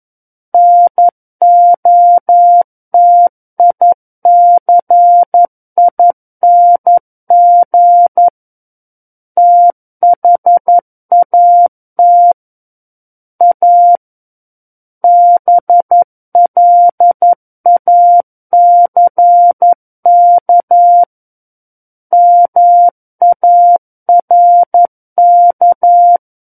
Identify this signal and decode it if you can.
Morse code, heard as a keyed tone, about 11 wpm, 700 Hz. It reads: NOTICING THAT A BLACK MARK
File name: 11 WPM